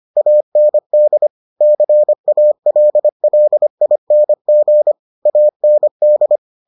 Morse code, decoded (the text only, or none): AND CALLING AND